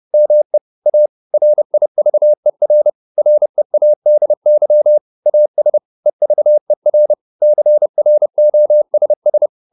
Morse code, heard as a keyed tone, about 30 wpm, 600 Hz. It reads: ME A RIVER READY AS EVER CROSS